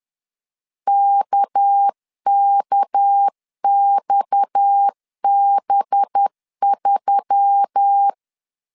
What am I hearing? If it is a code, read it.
Morse code, heard as a keyed tone, about 10 wpm, 800 Hz. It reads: KKXB3
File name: CW / DET / 1 9 1500C S B 10wpm 4439